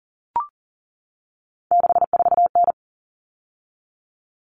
Morse code, heard as a keyed tone, about 40 wpm, 700 Hz. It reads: 64N
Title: All Cut Numbers - 3-Digit 40wpm